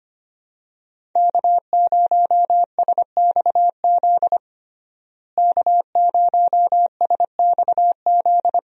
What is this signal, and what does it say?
Morse code, heard as a keyed tone, about 25 wpm, 700 Hz. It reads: K0SXZ K0SXZ
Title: USA Call Signs - 5 Characters - Repeated 3x - 25wpm